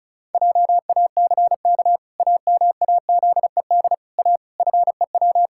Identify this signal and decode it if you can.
Morse code, heard as a keyed tone, about 34 wpm, 700 Hz. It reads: JACK AMAZED A FEW
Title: Pangram Challenge v2 - 35wpm